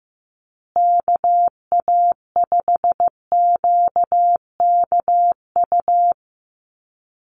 Morse code, heard as a keyed tone, about 15 wpm, 700 Hz. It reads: KA5QKU
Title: USA Call Signs Repeated 2x 15wpm